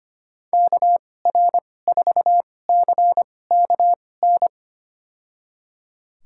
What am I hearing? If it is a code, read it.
Morse code, heard as a keyed tone, about 25 wpm, spaced as for 18 wpm, 700 Hz. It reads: KR4CKN
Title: USA Call Signs - 2x Character Spacing - 25wpm